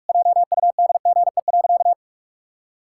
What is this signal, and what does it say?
Morse code, heard as a keyed tone, about 46 wpm, 700 Hz. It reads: JUDGE.
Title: Pangram Challenge v2 - 45wpm